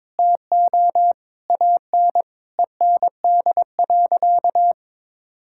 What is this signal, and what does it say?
Morse code, heard as a keyed tone, about 21 wpm, 700 Hz. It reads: TO AN END.
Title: Sentences from Top 1000 Words - Review Part 6 - 22wpm